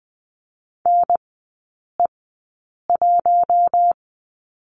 Morse code, heard as a keyed tone, about 20 wpm, spaced as for 8 wpm, 700 Hz. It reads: NE1
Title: 047 Lesson 9e - Number 4 - Three Characters 20wpm